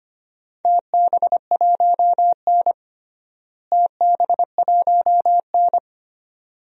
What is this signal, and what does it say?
Morse code, heard as a keyed tone, about 25 wpm, 700 Hz. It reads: TB1N TB1N
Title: DXCC-100 Call Signs - 4 Characters - Repeated 2x - 25wpm